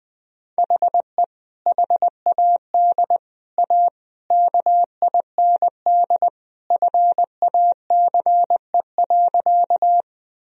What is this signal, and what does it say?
Morse code, heard as a keyed tone, about 20 wpm, 700 Hz. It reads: HE HAD A KIND FACE.
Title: Sentences from Top 1000 Words - Review Part 2 - 20wpm